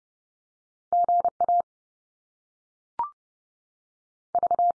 Morse code, heard as a keyed tone, about 30 wpm, 700 Hz. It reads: GA V
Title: US States 30WPM